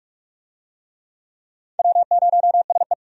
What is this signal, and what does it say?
Morse code, heard as a keyed tone, about 46 wpm, 700 Hz. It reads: W0SE
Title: USA Call Signs - 4 Characters - 45wpm